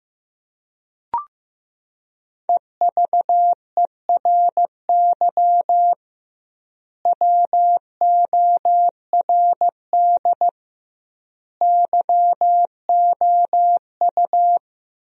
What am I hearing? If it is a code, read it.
Morse code, heard as a keyed tone, about 15 wpm, 700 Hz. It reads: EVERY WORD YOU
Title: 2x Word-Spacing - Sentences from Top 400 Words - 15wpm